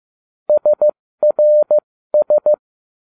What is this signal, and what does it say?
Morse code, heard as a keyed tone, about 15 wpm, spaced as for 13 wpm, 600 Hz. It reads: SRS